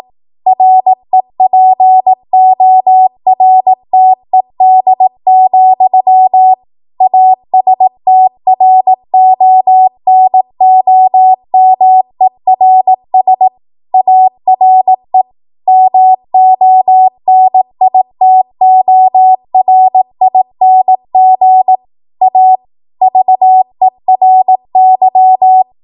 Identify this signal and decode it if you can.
Morse code, heard as a keyed tone, about 18 wpm, 750 Hz. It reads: REPORTED, ASTRONOMERS ARE MONITORING A VERY
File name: Morse / 18WPM